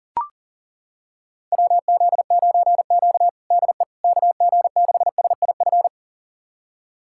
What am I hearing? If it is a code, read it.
Morse code, heard as a keyed tone, about 37 wpm, 700 Hz. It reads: WZ9Q DE KG6SIF
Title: USA Callsign DE Callsign 40wpm